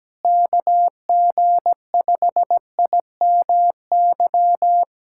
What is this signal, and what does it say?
Morse code, heard as a keyed tone, about 17 wpm, 700 Hz. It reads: KG5IMY
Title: USA Callsign DE Callsign - 6 Characters - 17wpm